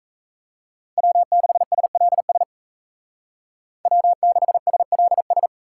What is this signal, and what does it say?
Morse code, heard as a keyed tone, about 33 wpm, 700 Hz. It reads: W6SLS W6SLS
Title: Speed Racing - USA 1x3 Call Signs 28wpm